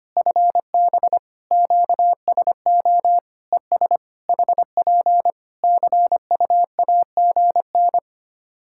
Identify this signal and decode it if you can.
Morse code, heard as a keyed tone, about 25 wpm, 700 Hz. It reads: FB QSO ES HP CUAGN
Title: Practice QSO - Exchange 4 - Ending - 25wpm